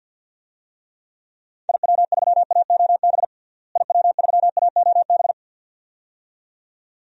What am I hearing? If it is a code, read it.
Morse code, heard as a keyed tone, about 53 wpm, 700 Hz. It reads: IW3AOB IW3AOB
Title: DXCC-100 Call Signs - 6 Characters - Repeated 2x - 50wpm